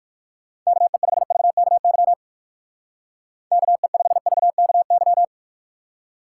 Morse code, heard as a keyed tone, about 59 wpm, 700 Hz. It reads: KE5VKY KE5VKY